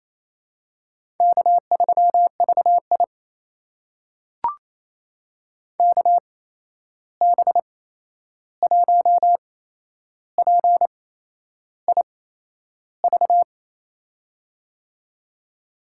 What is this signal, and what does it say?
Morse code, heard as a keyed tone, about 28 wpm, spaced as for 11 wpm, 700 Hz. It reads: K3VI KB1PIV
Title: USA Call Signs - 8x Character Spacing - 28wpm